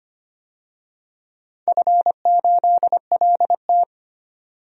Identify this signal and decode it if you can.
Morse code, heard as a keyed tone, about 25 wpm, 700 Hz. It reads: F8LT
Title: DXCC-100 Call Signs - 4 Characters - 25wpm